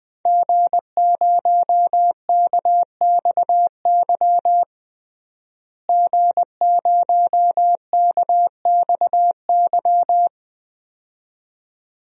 Morse code, heard as a keyed tone, about 20 wpm, 700 Hz. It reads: G0KXY G0KXY
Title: DXCC-100 Call Signs - 5 Characters - Repeated 2x - 20wpm